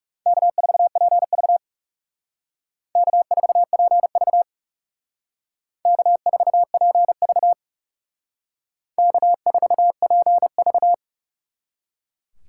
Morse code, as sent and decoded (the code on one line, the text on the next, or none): -.- ....- .--. ...- / -.- ....- .--. ...- / -.- ....- .--. ...- / -.- ....- .--. ...-
K4PV K4PV K4PV K4PV